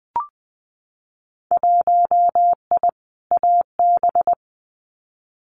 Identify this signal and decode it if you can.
Morse code, heard as a keyed tone, about 20 wpm, 700 Hz. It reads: 1I AB
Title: Winter Field Day Exchanges Encore 20wpm